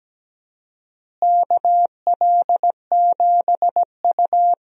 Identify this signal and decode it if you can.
Morse code, heard as a keyed tone, about 17 wpm, 700 Hz. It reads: KL7U
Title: DXCC-100 Call Signs - 4 Characters - Repeated 2x - 17wpm